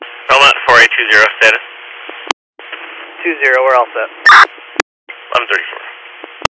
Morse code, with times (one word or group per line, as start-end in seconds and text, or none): none